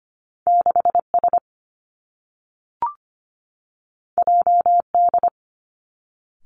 Morse code, heard as a keyed tone, about 25 wpm, 700 Hz. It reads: 6 S J D
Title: Two Challenging Letter-Number Combinations 25wpm